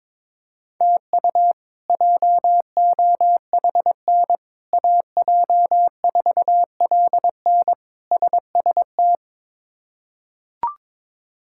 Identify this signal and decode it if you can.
Morse code, heard as a keyed tone, about 22 wpm, 700 Hz. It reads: TU JOHN AJ4LN SST
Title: K1USN SST QSO Practice - 22wpm